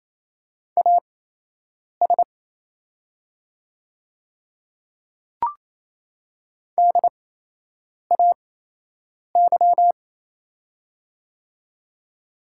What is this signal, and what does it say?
Morse code, heard as a keyed tone, about 28 wpm, spaced as for 8 wpm, 700 Hz. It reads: AS DAY
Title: 8x Character Spacing - Top 100 Words - 28wpm